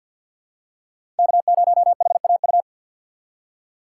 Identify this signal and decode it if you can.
Morse code, heard as a keyed tone, about 52 wpm, 700 Hz. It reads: K0HAU
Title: USA Call Signs - 5 Characters - 3x Character Spacing - 50wpm